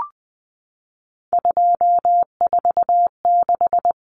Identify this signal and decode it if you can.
Morse code, heard as a keyed tone, about 20 wpm, 700 Hz. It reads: 246
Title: Common Cut Numbers - 3-Digit 20wpm